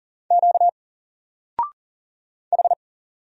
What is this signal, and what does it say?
Morse code, heard as a keyed tone, about 40 wpm, 700 Hz. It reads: Q H
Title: Single Letter-Number 40wpm